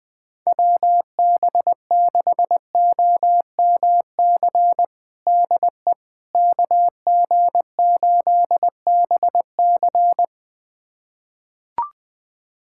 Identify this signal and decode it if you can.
Morse code, heard as a keyed tone, about 20 wpm, 700 Hz. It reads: WB6OMC DE KG8BC
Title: USA Callsign DE Callsign 20wpm